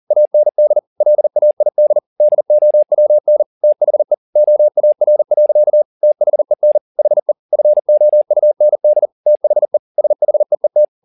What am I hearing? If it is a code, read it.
Morse code, heard as a keyed tone, about 40 wpm, 600 Hz. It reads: AND LAID DOWN THE OAR. THEN HE FOUND THE SHEET
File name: the old man and the sea 2 40wpm